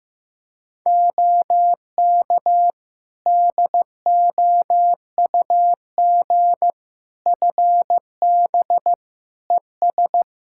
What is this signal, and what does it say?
Morse code, heard as a keyed tone, about 15 wpm, 700 Hz. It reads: OK DOUG FB ES
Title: Practice QSO - Exchange 2 - Info - 15wpm